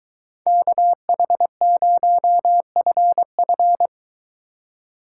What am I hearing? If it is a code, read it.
Morse code, heard as a keyed tone, about 22 wpm, 700 Hz. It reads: KH0FF